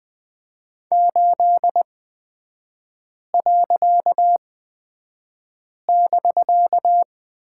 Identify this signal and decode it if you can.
Morse code, heard as a keyed tone, about 20 wpm, spaced as for 5 wpm, 700 Hz. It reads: 8.<BK>
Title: 255 Lesson 41e - Period - Three Characters 20wpm